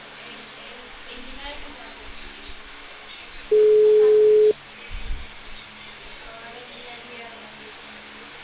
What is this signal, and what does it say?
Ringback tone (Europe, 425 Hz), with speech.